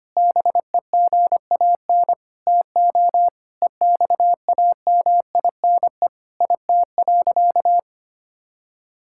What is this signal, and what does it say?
Morse code, heard as a keyed tone, about 25 wpm, 700 Hz. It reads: BEGAN TO EXAMINE IT.
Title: Sentences from Top 1600 Words 25wpm